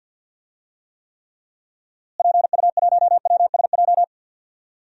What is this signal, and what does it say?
Morse code, heard as a keyed tone, about 50 wpm, 700 Hz. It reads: PU1WSJ
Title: DXCC-100 Call Signs - 6 Characters - Repeated 3x - 50wpm